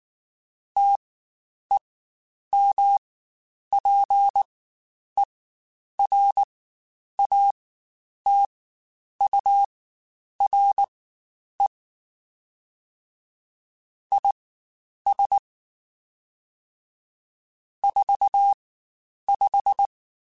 Morse code, heard as a keyed tone, about 19 wpm, spaced as for 9 wpm, 800 Hz. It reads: TEMPERATURE IS 45
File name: Spaced Characters / Spaced Characters-mp3s / Weather 3 17 sp @18